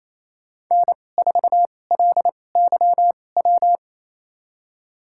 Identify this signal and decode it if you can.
Morse code, heard as a keyed tone, about 28 wpm, spaced as for 20 wpm, 700 Hz. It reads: N4LYW
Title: USA Call Signs - 5 Characters - 2x Character Spacing - 28wpm